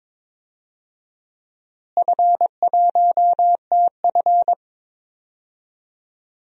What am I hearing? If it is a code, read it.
Morse code, heard as a keyed tone, about 22 wpm, 700 Hz. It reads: F1TF
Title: DXCC-100 Call Signs - 4 Characters - Repeated 2x - 22wpm